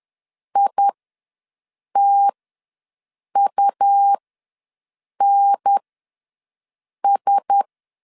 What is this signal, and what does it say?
Morse code, heard as a keyed tone, about 10 wpm, spaced as for 6 wpm, 800 Hz. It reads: ITUNS
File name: LEZIONI / 6 PKX0 / 6 PK ws 10wpm 3745